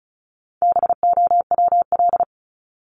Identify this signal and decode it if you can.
Morse code, heard as a keyed tone, about 34 wpm, 700 Hz. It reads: BOWL
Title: Top 1001-1250 Words 35wpm